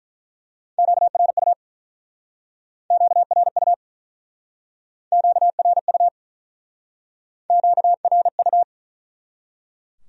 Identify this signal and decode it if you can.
Morse code, heard as a keyed tone, about 47 wpm, 700 Hz. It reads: QRU QRU QRU QRU